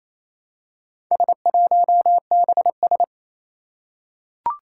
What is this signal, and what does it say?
Morse code, heard as a keyed tone, about 28 wpm, 700 Hz. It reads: S1BS
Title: Four Challenging Letter-Number Combinations 28wpm